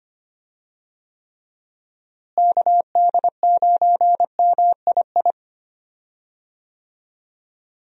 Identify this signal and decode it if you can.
Morse code, heard as a keyed tone, about 25 wpm, 700 Hz. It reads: KD9MII
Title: USA Call Signs - 6 Characters - Repeated 2x - 25wpm